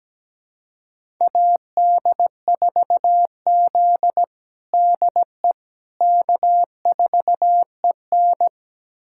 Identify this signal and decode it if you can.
Morse code, heard as a keyed tone, about 17 wpm, 700 Hz. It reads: AD4Z DE K4EN
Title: USA Callsign DE Callsign - 4 Characters - 17wpm